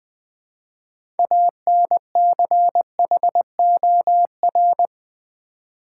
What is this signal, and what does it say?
Morse code, heard as a keyed tone, about 20 wpm, 700 Hz. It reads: ANCHOR
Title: Build A Word - 2 Letters At A Time - 6-Letter Words - 20wpm